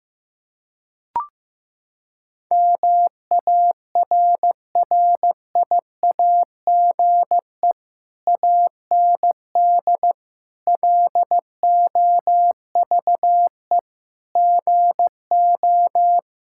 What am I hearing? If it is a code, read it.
Morse code, heard as a keyed tone, about 15 wpm, 700 Hz. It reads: MARRIAGE AND LOVE GO
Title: Sentences from Top 1000 Words - Review Part 5 - 15wpm